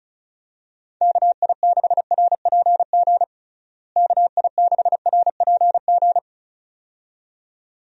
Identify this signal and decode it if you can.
Morse code, heard as a keyed tone, about 36 wpm, 700 Hz. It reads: KI6RPG KI6RPG
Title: USA Call Signs - 6 Characters - Repeated 2x - 35wpm